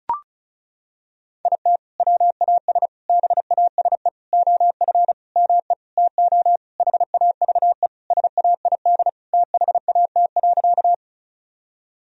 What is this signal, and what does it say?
Morse code, heard as a keyed tone, about 34 wpm, 700 Hz. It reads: IT WAS BASE OF ME TO HAVE SAID THAT.
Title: Sentences from Top 1000 Words - Review Part 1 - 35wpm